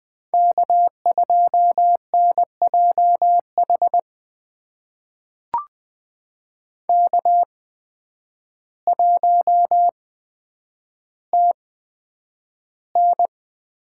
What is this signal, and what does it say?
Morse code, heard as a keyed tone, about 20 wpm, spaced as for 9 wpm, 700 Hz. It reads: K2NJH K1TN
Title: USA Call Signs - 5 Characters - 8x Character Spacing - 20wpm